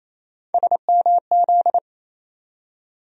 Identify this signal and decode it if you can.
Morse code, heard as a keyed tone, about 28 wpm, 700 Hz. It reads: SMZ
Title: USA Call Sign Postfixes - 3 Letters 28wpm